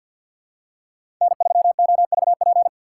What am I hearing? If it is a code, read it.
Morse code, heard as a keyed tone, about 51 wpm, 700 Hz. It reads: N3OVP
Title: USA Call Signs - 5 Characters - 4x Character Spacing - 50wpm